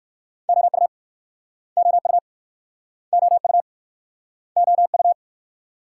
Morse code, heard as a keyed tone, about 62 wpm, 700 Hz. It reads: OU OU OU OU